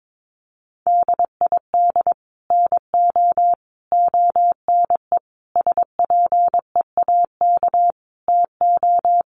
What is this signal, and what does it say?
Morse code, heard as a keyed tone, about 22 wpm, 700 Hz. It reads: DID NO ONE SPEAK TO
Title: Sentences from Top 1100 Words 22wpm